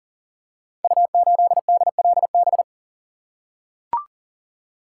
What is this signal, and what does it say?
Morse code, heard as a keyed tone, about 38 wpm, 700 Hz. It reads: U8DLB
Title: Code Groups - 5 Letter:Numbers - 40wpm